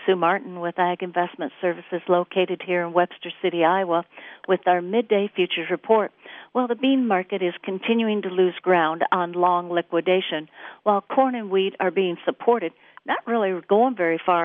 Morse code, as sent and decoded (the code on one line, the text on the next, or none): none